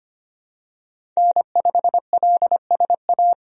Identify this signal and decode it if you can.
Morse code, heard as a keyed tone, about 25 wpm, 700 Hz. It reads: N5LSA
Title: USA Call Signs - 5 Characters - 25wpm